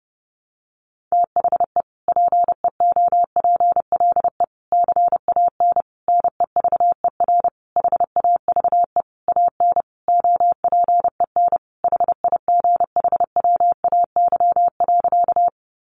Morse code, heard as a keyed tone, about 30 wpm, 700 Hz. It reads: THE PEOPLE CAN NEVER HAVE AN OPEN HIGHWAY.
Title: Sentences from Top 1400 Words 30wpm